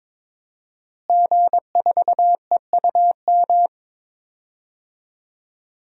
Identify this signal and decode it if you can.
Morse code, heard as a keyed tone, about 21 wpm, 700 Hz. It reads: G4EUM
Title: DXCC-100 Call Signs - 5 Characters - Repeated 3x - 22wpm